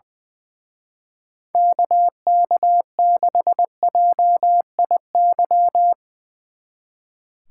Morse code, heard as a keyed tone, about 23 wpm, 700 Hz. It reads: KK6JIY
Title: Speed Racing - USA 2x3 Call Signs 20wpm